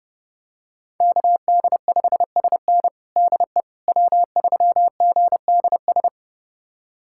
Morse code, heard as a keyed tone, about 32 wpm, 700 Hz. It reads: KD5SN DE W3GDS
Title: USA Callsign DE Callsign 30wpm